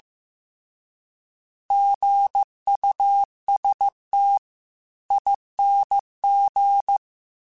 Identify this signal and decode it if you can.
Morse code, heard as a keyed tone, about 15 wpm, 800 Hz. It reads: GUSTING
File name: Alphabetical / Alphabetical mp3s / Gs @13